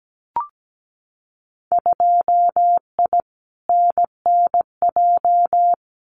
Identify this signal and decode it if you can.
Morse code, heard as a keyed tone, about 17 wpm, 700 Hz. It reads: 2I NNJ
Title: Winter Field Day Exchanges Encore 17wpm